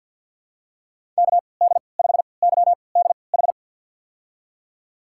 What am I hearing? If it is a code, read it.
Morse code, heard as a keyed tone, about 49 wpm, spaced as for 28 wpm, 700 Hz. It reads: KD5YDH
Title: USA Call Signs - 6 Characters - 3x Character Spacing - 50wpm